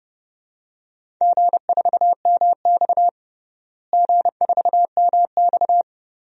Morse code, heard as a keyed tone, about 32 wpm, 700 Hz. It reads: G4MX G4MX
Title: DXCC-100 Call Signs - 4 Characters - Repeated 3x - 30wpm